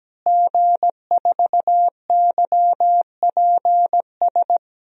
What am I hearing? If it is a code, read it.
Morse code, heard as a keyed tone, about 17 wpm, 700 Hz. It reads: G4YPS
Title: DXCC-100 Call Signs - 5 Characters - Repeated 2x - 17wpm